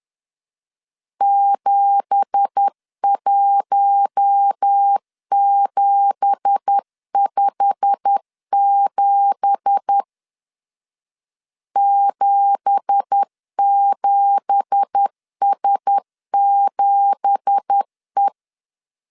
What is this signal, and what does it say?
Morse code, heard as a keyed tone, about 10 wpm, 800 Hz. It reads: 71757 77S7E
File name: LEZIONI / 9 2867 10wpm 2733